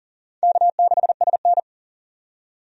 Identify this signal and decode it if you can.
Morse code, heard as a keyed tone, about 40 wpm, 700 Hz. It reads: K6SN
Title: USA Call Signs - 4 Characters - 40wpm